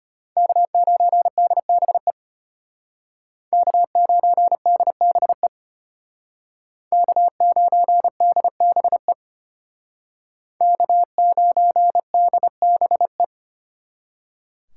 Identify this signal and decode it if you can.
Morse code, heard as a keyed tone, about 32 wpm, 700 Hz. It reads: K9DBE K9DBE K9DBE K9DBE